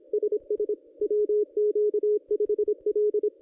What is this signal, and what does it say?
Morse code, heard as a keyed tone, about 24 wpm, 440 Hz. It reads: SS WQ5L
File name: contests / ss-cw-k5tr-op-n5rz -1400